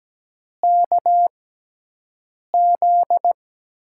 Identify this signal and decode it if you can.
Morse code, heard as a keyed tone, about 17 wpm, spaced as for 6 wpm, 700 Hz. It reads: KZ
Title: USA Call Signs - 6 Characters - 6x Character Spacing - 17wpm